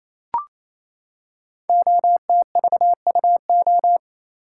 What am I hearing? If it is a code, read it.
Morse code, heard as a keyed tone, about 28 wpm, 700 Hz. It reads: OTVUO
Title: Code Groups - 5 Letter:Numbers - 28wpm